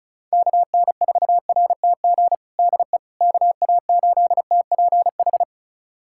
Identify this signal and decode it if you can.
Morse code, heard as a keyed tone, about 35 wpm, 700 Hz. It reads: KN4RTG DE KA8TPH
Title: USA Callsign DE Callsign - 6 Characters - 35wpm